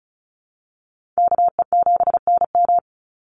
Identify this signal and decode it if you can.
Morse code, heard as a keyed tone, about 34 wpm, 700 Hz. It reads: KE7NM
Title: USA Call Signs 35wpm